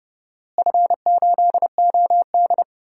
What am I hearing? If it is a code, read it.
Morse code, heard as a keyed tone, about 30 wpm, 700 Hz. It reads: F8OD
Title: DXCC-100 Call Signs - 4 Characters - Repeated 2x - 30wpm